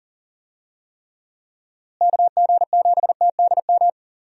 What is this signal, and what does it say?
Morse code, heard as a keyed tone, about 40 wpm, 700 Hz. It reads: KG7TDM
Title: USA Call Signs - 6 Characters - Repeated 2x - 40wpm